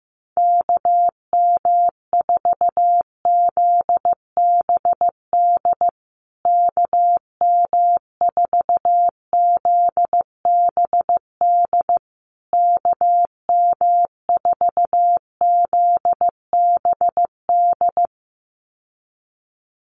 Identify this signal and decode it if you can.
Morse code, heard as a keyed tone, about 15 wpm, 700 Hz. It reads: KM4ZBD KM4ZBD KM4ZBD
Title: USA Call Signs Repeated 3x 15wpm